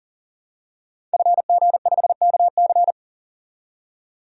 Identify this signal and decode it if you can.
Morse code, heard as a keyed tone, about 40 wpm, 700 Hz. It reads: FG5KC
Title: DXCC-100 Call Signs - 5 Characters - 40wpm